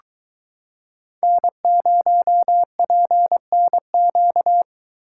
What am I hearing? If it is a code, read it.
Morse code, heard as a keyed tone, about 23 wpm, 700 Hz. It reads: N0PNQ